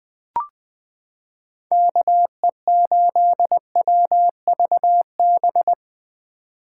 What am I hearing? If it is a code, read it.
Morse code, heard as a keyed tone, about 21 wpm, 700 Hz. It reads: KE8WVB
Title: 2023 POTA Activator Call Signs - 20wpm